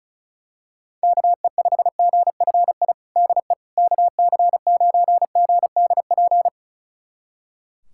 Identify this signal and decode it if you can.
Morse code, heard as a keyed tone, about 36 wpm, 700 Hz. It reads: KE5GFI DE KC9GDP
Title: USA Callsign DE Callsign 35wpm